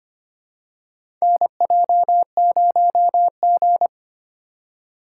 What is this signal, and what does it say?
Morse code, heard as a keyed tone, about 25 wpm, 700 Hz. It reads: NJ0G
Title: USA Call Signs - 4 Characters - 4x Character Spacing - 25wpm